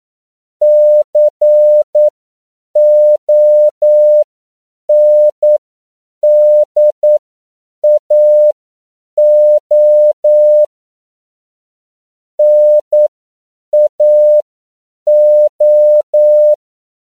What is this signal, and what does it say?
Morse code, heard as a keyed tone, about 9 wpm, spaced as for 7 wpm, 600 Hz. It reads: CONDAO NAO